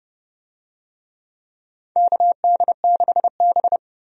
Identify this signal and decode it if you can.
Morse code, heard as a keyed tone, about 31 wpm, 700 Hz. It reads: KD6B
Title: USA Call Signs - 4 Characters - 30wpm